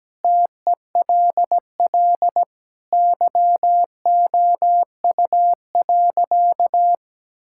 Morse code, heard as a keyed tone, about 17 wpm, 700 Hz. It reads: TELL YOU.
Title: Sentences from Top 1600 Words 17wpm